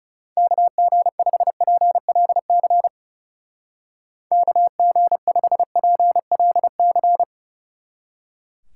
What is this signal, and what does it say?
Morse code, heard as a keyed tone, about 37 wpm, 700 Hz. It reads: KG5PLC KG5PLC
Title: Speed Racing - USA 2x3 Call Signs 30wpm